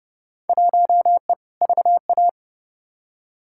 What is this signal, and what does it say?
Morse code, heard as a keyed tone, about 30 wpm, 700 Hz. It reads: 1E VA
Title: Field Day Exchanges - 30wpm